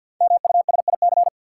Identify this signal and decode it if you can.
Morse code, heard as a keyed tone, about 53 wpm, 700 Hz. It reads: MUSIC
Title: Sentences from Top 300 Words - New Word Drill - 50wpm